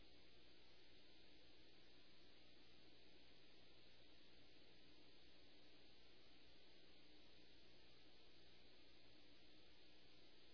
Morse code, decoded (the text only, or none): none